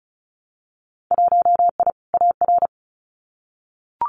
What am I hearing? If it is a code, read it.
Morse code, heard as a keyed tone, about 36 wpm, 700 Hz. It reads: 1I AR
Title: Winter Field Day Exchanges Encore 35wpm